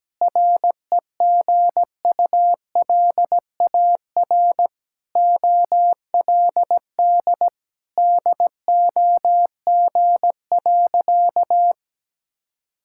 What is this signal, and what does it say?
Morse code, heard as a keyed tone, about 17 wpm, 700 Hz. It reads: REGULAR OLD DOG.
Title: Sentences from Top 1000 Words - Review Part 6 - 17wpm